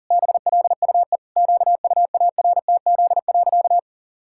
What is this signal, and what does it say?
Morse code, heard as a keyed tone, about 37 wpm, 700 Hz. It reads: BLUE QUARTZ.
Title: Pangram Challenge v2 - 40wpm